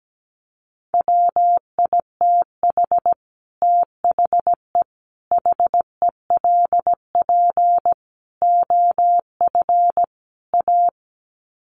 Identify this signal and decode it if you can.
Morse code, heard as a keyed tone, about 17 wpm, 700 Hz. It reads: WITH THE HELP OF A
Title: Sets of 5 Words 17wpm